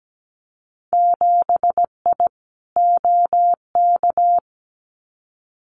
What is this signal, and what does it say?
Morse code, heard as a keyed tone, about 17 wpm, 700 Hz. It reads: 7I OK
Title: Winter Field Day Exchanges Encore 17wpm